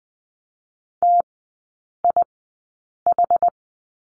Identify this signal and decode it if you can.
Morse code, heard as a keyed tone, about 20 wpm, spaced as for 8 wpm, 700 Hz. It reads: TIH
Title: 060 Lesson 11e - Letter h - Three Characters 20wpm